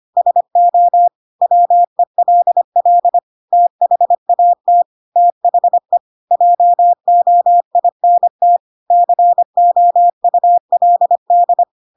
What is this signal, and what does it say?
Morse code, heard as a keyed tone, about 25 wpm, 700 Hz. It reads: SO WELL THAT THE JOINT COULD